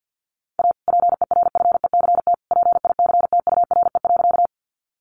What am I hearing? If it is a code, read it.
Morse code, heard as a keyed tone, about 50 wpm, 700 Hz. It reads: A PERFECT PICTURE.
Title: Sentences from Top 1000 Words 50wpm